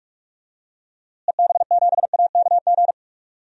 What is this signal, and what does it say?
Morse code, heard as a keyed tone, about 46 wpm, 700 Hz. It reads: EB7AKG
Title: DXCC-100 Call Signs - 6 Characters - Repeated 3x - 45wpm